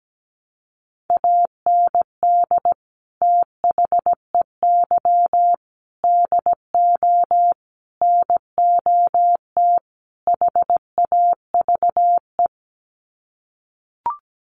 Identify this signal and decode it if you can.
Morse code, heard as a keyed tone, about 17 wpm, 700 Hz. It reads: AND THEY DO NOT HAVE
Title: Sets of 5 Words Encore2 17wpm